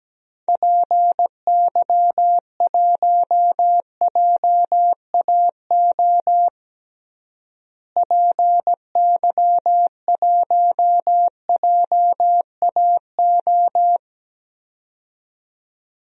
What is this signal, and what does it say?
Morse code, heard as a keyed tone, about 17 wpm, 700 Hz. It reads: PY1JAO PY1JAO
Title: DXCC-100 Call Signs - 6 Characters - Repeated 2x - 17wpm